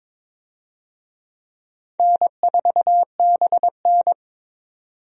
Morse code, heard as a keyed tone, about 22 wpm, 700 Hz. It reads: N4BN
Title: USA Call Signs - 4 Characters - 22wpm